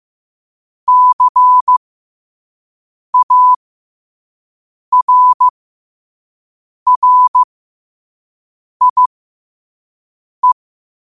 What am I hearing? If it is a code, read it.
Morse code, heard as a keyed tone, about 15 wpm, spaced as for 5 wpm, 1000 Hz. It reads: CARRIE
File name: qotdfiles / 5wpm